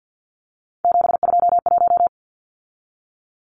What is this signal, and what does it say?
Morse code, heard as a keyed tone, about 50 wpm, 700 Hz. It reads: 721
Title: Numbers - 3-Digit 50wpm